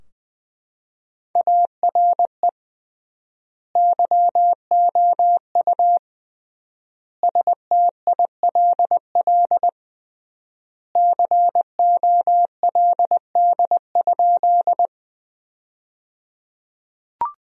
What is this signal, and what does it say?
Morse code, heard as a keyed tone, about 21 wpm, 700 Hz. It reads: ARE YOU STILL COLD?
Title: 3x Word-Spacing - Sentences from Top 400 Words - 20wpm